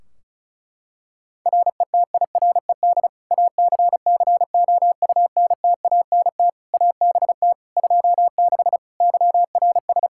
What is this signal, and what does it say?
Morse code, heard as a keyed tone, about 34 wpm, 700 Hz. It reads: RETIRED ACCOUNTANT ABT 26 YRS
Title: Practice QSO - Exchange 3 - More Info - 35wpm